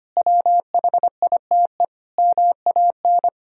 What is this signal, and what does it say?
Morse code, heard as a keyed tone, about 25 wpm, 700 Hz. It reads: WHITE MAN
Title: Sentences from Top 1000 Words - Review Part 1 - 25wpm